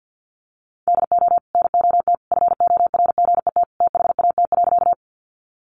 Morse code, heard as a keyed tone, about 50 wpm, 700 Hz. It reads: DO NOT FORGET THAT.
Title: Sentences from Top 1200 Words 50wpm